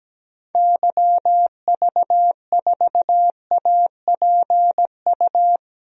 Morse code, heard as a keyed tone, about 17 wpm, 700 Hz. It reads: YV4APU
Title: DXCC-100 Call Signs - 6 Characters - 17wpm